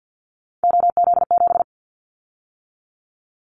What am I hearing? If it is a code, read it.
Morse code, heard as a keyed tone, about 50 wpm, 700 Hz. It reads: OZ7
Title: Callsign Prefixes for Top 50 DXCC - 50wpm